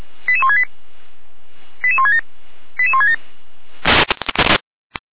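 Five-tone selective call (ZVEI), heard as quick runs of five tones, tones of about 70 ms. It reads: 88168 88167 88167
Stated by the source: ZVEI five-tone call received by SDR, from the feuerwehr-orlamuende.org fire brigade audio archive.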